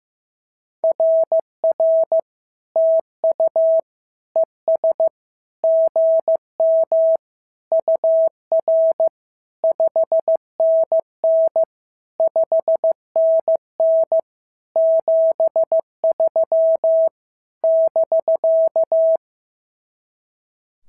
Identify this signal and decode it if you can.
Morse code, heard as a keyed tone, about 14 wpm, 650 Hz. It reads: RR TU ES GM UR 5NN 5NN 73 <BK>
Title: SOTA QSO Practice 15wpm